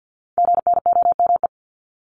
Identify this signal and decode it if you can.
Morse code, heard as a keyed tone, about 50 wpm, 700 Hz. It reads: GNOME
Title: Multi-letter Phonemes 50wpm